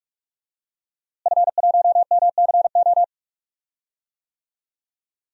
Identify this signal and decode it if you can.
Morse code, heard as a keyed tone, about 44 wpm, 700 Hz. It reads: F1MCO
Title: DXCC-100 Call Signs - 5 Characters - Repeated 2x - 45wpm